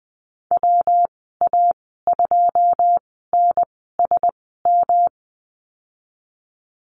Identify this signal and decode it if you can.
Morse code, heard as a keyed tone, about 20 wpm, spaced as for 15 wpm, 700 Hz. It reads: WA2NSM
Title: 122 Lesson 20g - Number 3 - Call Signs 20wpm